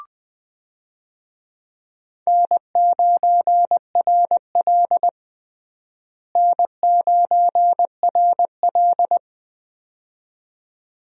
Morse code, heard as a keyed tone, about 20 wpm, 700 Hz. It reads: N9RL N9RL
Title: USA Call Signs - 4 Characters - Repeated 2x - 20wpm